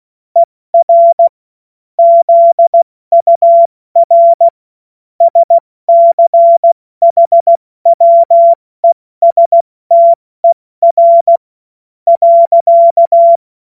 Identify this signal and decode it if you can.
Morse code, heard as a keyed tone, about 16 wpm, spaced as for 13 wpm, 680 Hz. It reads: ER ZUR SCHWESTER.
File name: Morsekurs / Witze / Witz 07. September 2021, 70 F80